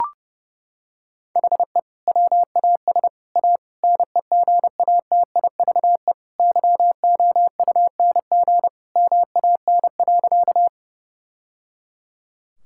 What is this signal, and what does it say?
Morse code, heard as a keyed tone, about 32 wpm, 700 Hz. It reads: HE WAS A NEGATIVE YOUNG MAN.
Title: Sentences from Top 1000 Words - Review Part 6 - 30wpm